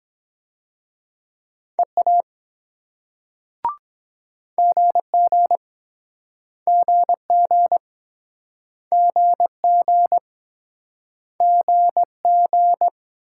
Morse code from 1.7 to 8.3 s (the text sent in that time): EA GG GG